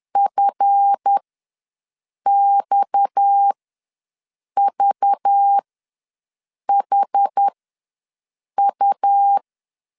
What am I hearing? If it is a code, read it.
Morse code, heard as a keyed tone, about 10 wpm, spaced as for 6 wpm, 800 Hz. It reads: FXVHU
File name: LEZIONI / 7 GF9Z / 7 GF ws 10wpm 3918